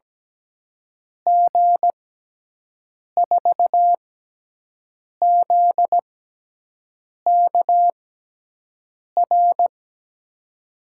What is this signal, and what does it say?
Morse code, heard as a keyed tone, about 17 wpm, spaced as for 6 wpm, 700 Hz. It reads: G4ZKR
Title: USA Call Signs - 6 Characters - 6x Character Spacing - 17wpm